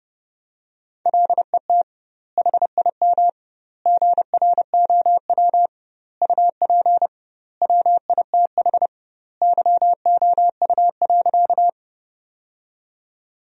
Morse code, heard as a keyed tone, about 30 wpm, 700 Hz. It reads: LET HIM GROW UP WITH YOU.
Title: 2x Word-Spacing - Sentences from Top 300 Words - 30wpm